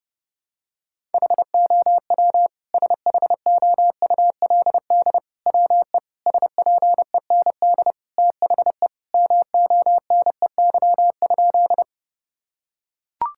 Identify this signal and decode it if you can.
Morse code, heard as a keyed tone, about 31 wpm, 700 Hz. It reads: HOW SHOULD WE SPEND THE MONEY?
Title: Sentences from Top 1000 Words - Review Part 5 - 30wpm